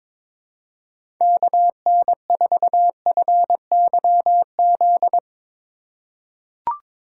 Morse code, heard as a keyed tone, about 22 wpm, 700 Hz. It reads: KN4FYZ
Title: USA Call Signs - 6 Characters - 3x Character Spacing - 22wpm